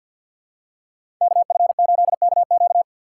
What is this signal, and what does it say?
Morse code, heard as a keyed tone, about 50 wpm, 700 Hz. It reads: KF8KQ
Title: USA Call Signs - 5 Characters - 2x Character Spacing - 50wpm